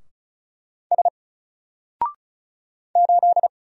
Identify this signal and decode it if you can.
Morse code, heard as a keyed tone, about 33 wpm, 700 Hz. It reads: S 8
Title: Single Letter-Number Alt 35wpm